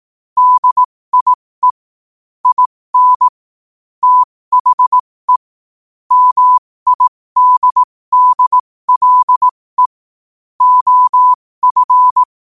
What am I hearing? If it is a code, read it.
Morse code, heard as a keyed tone, about 18 wpm, spaced as for 15 wpm, 1000 Hz. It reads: DIE IN THE MIDDLE OF